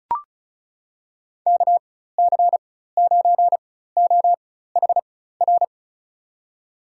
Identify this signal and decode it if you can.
Morse code, heard as a keyed tone, about 34 wpm, spaced as for 16 wpm, 700 Hz. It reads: KC9OHR
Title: USA Call Signs - 6 Characters - 4x Character Spacing - 35wpm